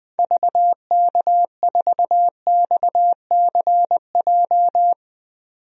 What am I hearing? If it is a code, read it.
Morse code, heard as a keyed tone, about 20 wpm, 700 Hz. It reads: VK4XCJ
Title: DXCC-100 Call Signs - 6 Characters - 20wpm